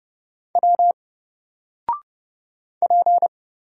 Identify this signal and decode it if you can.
Morse code, heard as a keyed tone, about 29 wpm, 700 Hz. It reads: W P
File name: Single-Letter-Number-30wpm